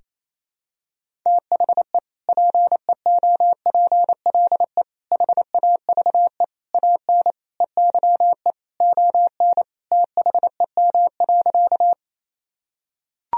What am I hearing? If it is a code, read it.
Morse code, heard as a keyed tone, about 28 wpm, 700 Hz. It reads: THE PEOPLE HAVE AN EYE ON THEM.
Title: Sentences from Top 1000 Words - Review Part 1 - 28wpm